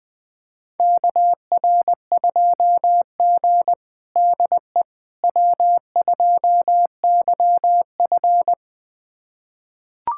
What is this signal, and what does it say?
Morse code, heard as a keyed tone, about 20 wpm, 700 Hz. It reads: KR2G DE W2YF
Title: USA Callsign DE Callsign - 4 Characters - 20wpm